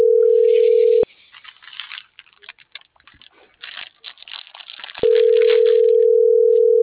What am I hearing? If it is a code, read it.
Ringback tone (North America), 440 + 480 Hz.